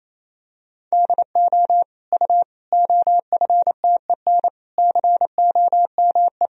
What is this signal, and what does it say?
Morse code, heard as a keyed tone, about 28 wpm, 700 Hz. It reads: DO U OFTEN COME
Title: Sentences from Top 300 Words and Abbreviations 28wpm